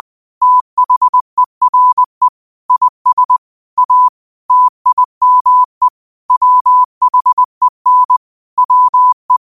Morse code, heard as a keyed tone, about 20 wpm, 1000 Hz. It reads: THERE IS A TIME WHEN WE